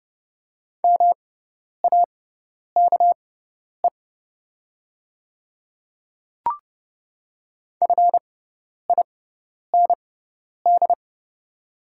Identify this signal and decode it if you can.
Morse code, heard as a keyed tone, about 30 wpm, spaced as for 10 wpm, 700 Hz. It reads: MAKE FIND
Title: 6x Character Spacing - Top 100 Words - 30wpm